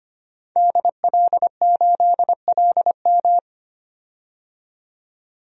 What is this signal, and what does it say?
Morse code, heard as a keyed tone, about 25 wpm, 700 Hz. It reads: DL8LM
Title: DXCC-100 Call Signs - 5 Characters - 25wpm